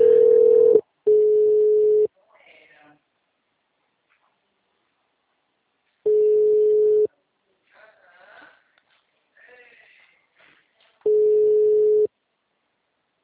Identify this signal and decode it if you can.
Ringback tone (Europe, 425 Hz).